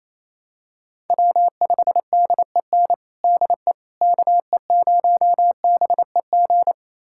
Morse code, heard as a keyed tone, about 28 wpm, 700 Hz. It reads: W5DEN DE KE0BEG
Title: USA Callsign DE Callsign 28wpm